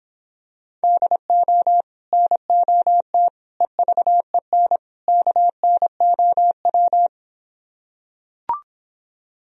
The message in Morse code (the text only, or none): DO NOT EVEN KNOW